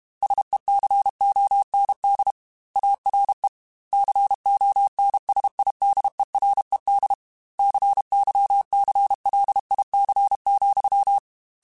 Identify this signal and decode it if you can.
Morse code, heard as a keyed tone, about 32 wpm, 800 Hz. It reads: SECOND ARE CONSIDERED CYCLIC,
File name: Technical Subjects / Technical Subjects-mp3s / XL @30